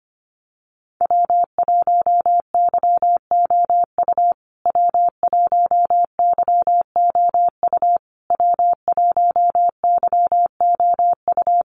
Morse code, heard as a keyed tone, about 25 wpm, 700 Hz. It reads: W1YOU W1YOU W1YOU
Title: USA Call Signs Repeated 3x 25wpm